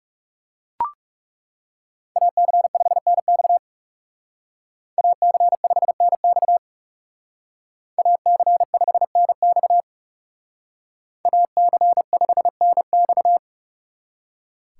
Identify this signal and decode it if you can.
Morse code, heard as a keyed tone, about 35 wpm, 700 Hz. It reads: AC5NX AC5NX AC5NX AC5NX
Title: Speed Racing - USA 2x2 Call Signs 30wpm